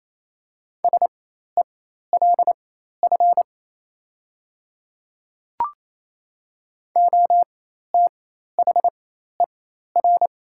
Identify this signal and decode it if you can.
Morse code, heard as a keyed tone, about 28 wpm, spaced as for 13 wpm, 700 Hz. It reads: SELF OTHER
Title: 4x Character Spacing - Top 200 Words - 28wpm